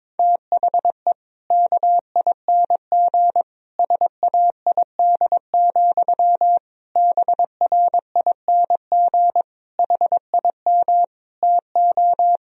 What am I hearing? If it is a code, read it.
Morse code, heard as a keyed tone, about 22 wpm, 700 Hz. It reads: THE KING SAID, BRING HIM TO
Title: Sentences from Top 1000 Words - Review Part 3 - 22wpm